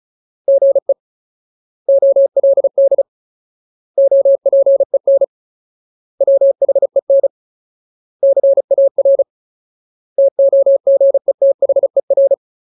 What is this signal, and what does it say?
Morse code, heard as a keyed tone, about 34 wpm, 550 Hz. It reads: GE OLD OPEN WHEN CAR TOGETHER
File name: Ultimate Word List / 35WPM Ultimate